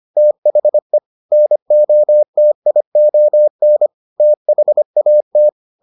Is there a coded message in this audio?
Yes: Morse code from 0.2 to 5.5 s, heard as a keyed tone, about 25 wpm, 600 Hz. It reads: THE NOTION THAT